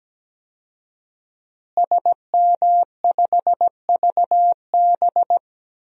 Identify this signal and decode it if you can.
Morse code, heard as a keyed tone, about 17 wpm, 700 Hz. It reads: SM5VB
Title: DXCC-100 Call Signs - 5 Characters - Repeated 3x - 17wpm